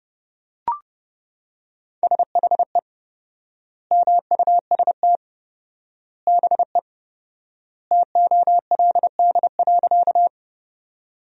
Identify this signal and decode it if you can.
Morse code, heard as a keyed tone, about 32 wpm, 700 Hz. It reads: SHE MUST BE TOLD.
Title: 4x Word-Spacing - Sentences from Top 400 Words - 30wpm